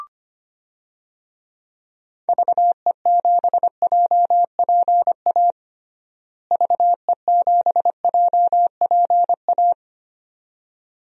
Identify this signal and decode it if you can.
Morse code, heard as a keyed tone, about 25 wpm, 700 Hz. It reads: VE7JPA VE7JPA
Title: DXCC-100 Call Signs - 6 Characters - Repeated 2x - 25wpm